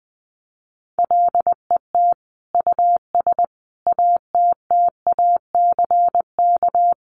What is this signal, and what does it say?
Morse code, heard as a keyed tone, about 20 wpm, 700 Hz. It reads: LET US ATTACK
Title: Sentences from Top 1500 Words 20wpm